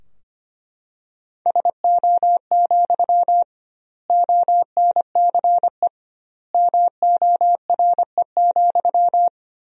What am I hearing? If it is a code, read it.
Morse code, heard as a keyed tone, about 25 wpm, 700 Hz. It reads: SO, ONCE MORE,
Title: 2x Word-Spacing - Sentences from Top 300 Words - 25wpm